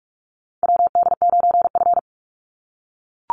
Morse code, heard as a keyed tone, about 44 wpm, 700 Hz. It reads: WD9F
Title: NA SOTA Chaser Callsigns 45wpm